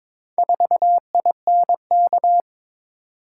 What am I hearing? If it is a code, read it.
Morse code, heard as a keyed tone, about 22 wpm, 700 Hz. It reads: 4INK
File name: practice / 22wpm / Four Letter-Number Combinations - 22wpm